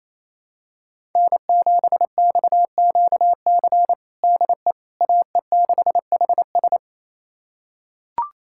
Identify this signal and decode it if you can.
Morse code, heard as a keyed tone, about 28 wpm, 700 Hz. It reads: N7XQC DE AE6HS
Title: USA Callsign DE Callsign - 5 Characters - 28wpm